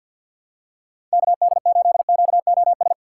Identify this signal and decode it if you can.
Morse code, heard as a keyed tone, about 47 wpm, 700 Hz. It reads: KD8QOS
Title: USA Call Signs - 6 Characters - 4x Character Spacing - 50wpm